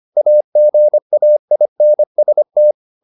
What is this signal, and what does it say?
Morse code, heard as a keyed tone, about 25 wpm, 600 Hz. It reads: AGAINST
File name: the old man and the sea part 1 25wpm